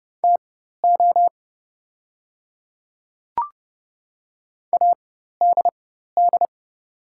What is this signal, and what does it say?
Morse code, heard as a keyed tone, about 30 wpm, spaced as for 14 wpm, 700 Hz. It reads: TO ADD
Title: 4x Character Spacing - Top 200 Words - 30wpm